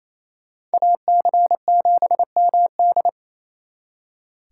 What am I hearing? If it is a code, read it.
Morse code, heard as a keyed tone, about 36 wpm, 700 Hz. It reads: AC7MD